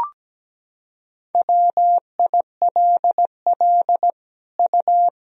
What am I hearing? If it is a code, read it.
Morse code, heard as a keyed tone, about 17 wpm, 700 Hz. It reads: WILL U
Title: Sentences from Top 200 Words and Abbreviations 17wpm